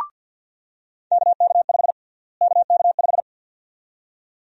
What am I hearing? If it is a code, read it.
Morse code, heard as a keyed tone, about 50 wpm, 700 Hz. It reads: KK5 KK5
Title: USA Call Sign Prefixes - Repeated 2x - 50wpm